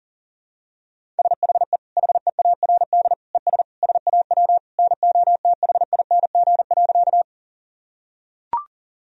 Morse code, heard as a keyed tone, about 40 wpm, 700 Hz. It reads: SHE HEARD ES SAW NOTHING.
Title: Sentences from Top 500 Words and Abbreviations 40wpm